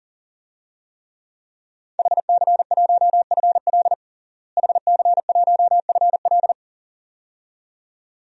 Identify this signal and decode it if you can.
Morse code, heard as a keyed tone, about 40 wpm, 700 Hz. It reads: HC1FL HC1FL
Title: DXCC-100 Call Signs - 5 Characters - Repeated 2x - 40wpm